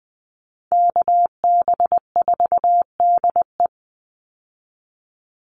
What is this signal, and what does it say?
Morse code, heard as a keyed tone, about 19 wpm, 700 Hz. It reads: KB4DE
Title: Call Signs - 20wpm